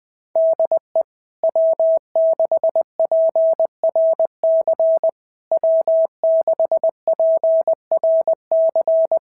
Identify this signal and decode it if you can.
Morse code, heard as a keyed tone, about 20 wpm, 650 Hz. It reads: DE W6PRC W6PRC
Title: SOTA QSO Practice 20wpm